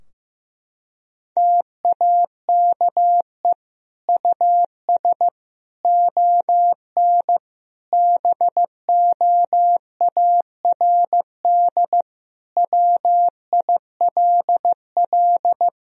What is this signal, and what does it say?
Morse code, heard as a keyed tone, about 15 wpm, 700 Hz. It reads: TAKE US ON BOARD WILL
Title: Sentences from Top 1000 Words - Review Part 4 - 15wpm